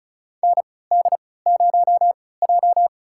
Morse code, heard as a keyed tone, about 34 wpm, spaced as for 20 wpm, 700 Hz. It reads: ND0J
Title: USA Call Signs - 3x Character Spacing - 35wpm